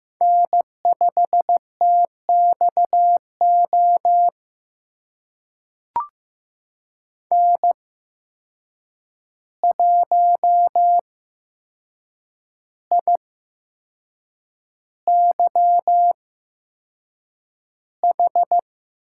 Morse code, heard as a keyed tone, about 15 wpm, spaced as for 6 wpm, 700 Hz. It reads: N5TXO N1IYH
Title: USA Call Signs - 5 Characters - 8x Character Spacing - 15wpm